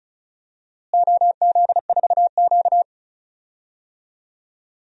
Morse code, heard as a keyed tone, about 36 wpm, 700 Hz. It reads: OZ4Q
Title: DXCC-100 Call Signs - 4 Characters - Repeated 2x - 35wpm